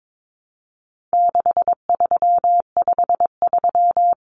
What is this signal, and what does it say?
Morse code, heard as a keyed tone, about 22 wpm, 700 Hz. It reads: 6353
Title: Numbers - 4-Digit 22wpm